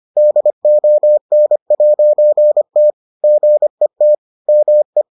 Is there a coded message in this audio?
Yes: Morse code from 0.2 to 5.0 s, heard as a keyed tone, about 25 wpm, 600 Hz. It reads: DON'T GET ME